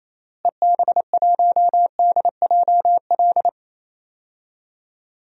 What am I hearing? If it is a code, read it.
Morse code, heard as a keyed tone, about 27 wpm, 700 Hz. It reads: EB1DJL
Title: DXCC-100 Call Signs - 6 Characters - Repeated 2x - 28wpm